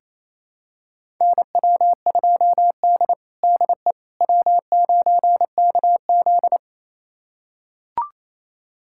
Morse code, heard as a keyed tone, about 27 wpm, 700 Hz. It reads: NW2D DE W9KZ
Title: USA Callsign DE Callsign - 4 Characters - 28wpm